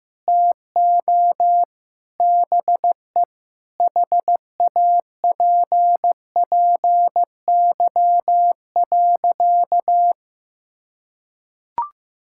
Morse code, heard as a keyed tone, about 16 wpm, 700 Hz. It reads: TO BE HAPPY.
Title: Sentences from Top 1000 Words - Review Part 5 - 15wpm